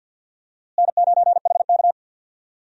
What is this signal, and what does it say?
Morse code, heard as a keyed tone, about 53 wpm, 700 Hz. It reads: N9HK
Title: USA Call Signs - 4 Characters - Repeated 3x - 50wpm